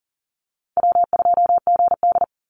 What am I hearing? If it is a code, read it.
Morse code, heard as a keyed tone, about 40 wpm, 700 Hz. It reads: W2GD
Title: Call Signs - 40wpm